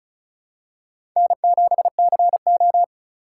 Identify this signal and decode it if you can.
Morse code, heard as a keyed tone, about 36 wpm, 700 Hz. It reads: N7CO
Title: USA Call Signs - 4 Characters - 2x Character Spacing - 35wpm